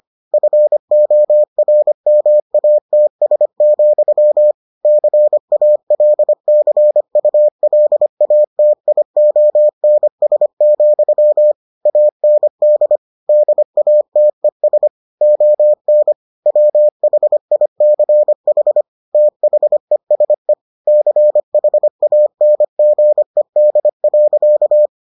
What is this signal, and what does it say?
Morse code, heard as a keyed tone, about 25 wpm, 600 Hz. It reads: FORMATS, CALCULATIONS, AND DATES ON WHICH THESE CHANGED.